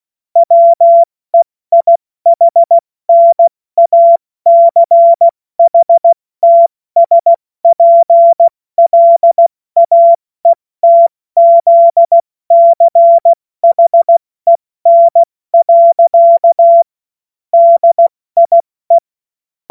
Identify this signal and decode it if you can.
Morse code, heard as a keyed tone, about 16 wpm, 680 Hz. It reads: WEIHNACHTSPLAETZCHEN. DIE
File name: Morsekurs / Witze / Witz 17. Januar 2023 70 F80